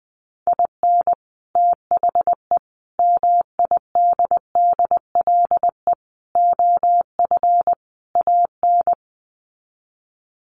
Morse code, heard as a keyed tone, about 20 wpm, 700 Hz. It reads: IN THE MIDDLE OF AN
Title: Sets of 5 Words Encore2 20wpm